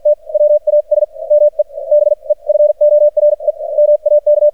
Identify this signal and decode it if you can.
Morse code, heard as a keyed tone, about 47 wpm, spaced as for 40 wpm, 600 Hz. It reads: T WAS ME BEFORE MAK
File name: archived / CRSnet